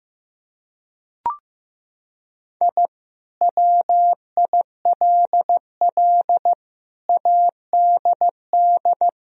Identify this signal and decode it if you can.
Morse code, heard as a keyed tone, about 15 wpm, 700 Hz. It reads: I WILL ADD
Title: Sentences from Top 1000 Words - Review Part 1 - 15wpm